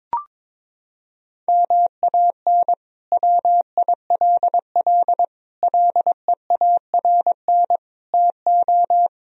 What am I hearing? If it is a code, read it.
Morse code, heard as a keyed tone, about 22 wpm, 700 Hz. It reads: MAN WILL LEARN TO
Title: Sentences from Top 1000 Words - Review Part 2 - 22wpm